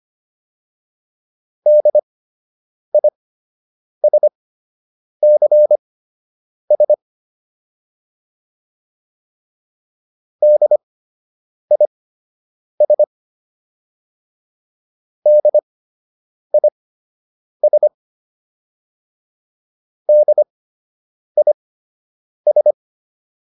Morse code, heard as a keyed tone, about 25 wpm, spaced as for 8 wpm, 600 Hz. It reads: DISCS DIS DIS DIS